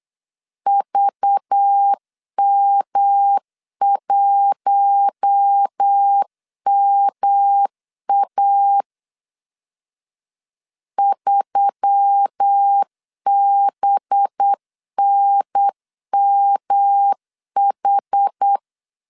Morse code, heard as a keyed tone, about 8 wpm, 800 Hz. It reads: VM1MA 3BNMH
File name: LEZIONI / 6 PKX0 / 6 P 8wpm 3025